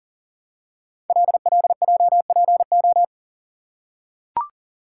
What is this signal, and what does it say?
Morse code, heard as a keyed tone, about 38 wpm, 700 Hz. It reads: LLJPO
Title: Code Groups - 5 Letters - 40wpm